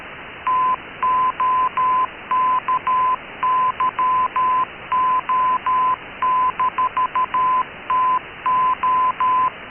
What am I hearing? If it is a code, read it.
Morse code, heard as a keyed tone, about 13 wpm, 1010 Hz. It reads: TOKYO-TO
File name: KKS-1 cw 21jul2014 1425z